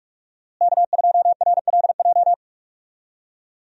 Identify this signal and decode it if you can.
Morse code, heard as a keyed tone, about 46 wpm, 700 Hz. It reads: K2RLJ
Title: USA Call Signs - 5 Characters - Repeated 2x - 45wpm